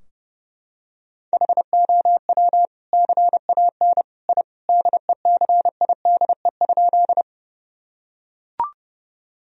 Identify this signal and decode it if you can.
Morse code, heard as a keyed tone, about 30 wpm, 700 Hz. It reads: HOW CAN I DECIDE?
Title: Sentences from Top 1000 Words - Review Part 3 - 30wpm